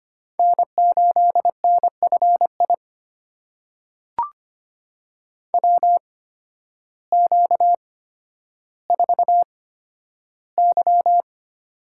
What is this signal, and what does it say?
Morse code, heard as a keyed tone, about 25 wpm, spaced as for 12 wpm, 700 Hz. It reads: N8NFI WQ4Y
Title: USA Call Signs - 8x Character Spacing - 25wpm